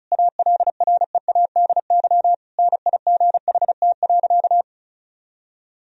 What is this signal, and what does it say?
Morse code, heard as a keyed tone, about 34 wpm, 700 Hz. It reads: ALREADY NIGHT.
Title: Sentences from Top 1000 Words - Review Part 1 - 35wpm